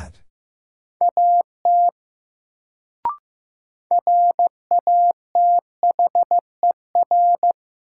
Morse code, heard as a keyed tone, about 15 wpm, 700 Hz. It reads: AT RATHER
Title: Sets of 2 Words - New Word Drill - 15wpm